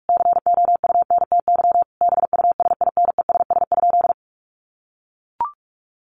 Morse code, heard as a keyed tone, about 44 wpm, 700 Hz. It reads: COUNTY BUSINESS?
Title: Sentences from Top 1200 Words 45wpm